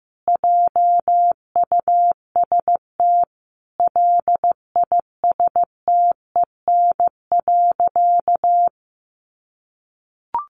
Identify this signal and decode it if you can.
Morse code, heard as a keyed tone, about 15 wpm, 700 Hz. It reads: JUST LISTEN.
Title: Sentences from Top 1300 Words 15wpm